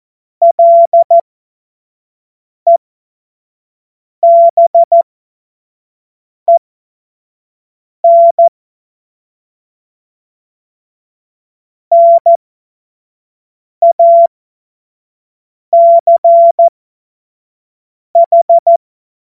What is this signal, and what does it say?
Morse code, heard as a keyed tone, about 14 wpm, spaced as for 5 wpm, 680 Hz. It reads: LEBEN NACH
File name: Morsekurs / Witze / Witz 13. Dezember 2022 25 F70